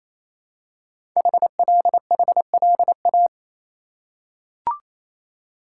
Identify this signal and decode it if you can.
Morse code, heard as a keyed tone, about 28 wpm, 700 Hz. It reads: HLHLA
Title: Code Groups - 5 Letters - 28wpm